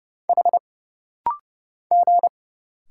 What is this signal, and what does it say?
Morse code, heard as a keyed tone, about 30 wpm, 700 Hz. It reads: H G
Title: Single Letter-Number Alt - Rapid-Fire 30wpm